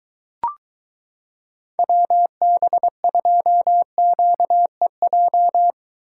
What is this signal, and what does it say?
Morse code, heard as a keyed tone, about 22 wpm, 700 Hz. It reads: WB2QEJ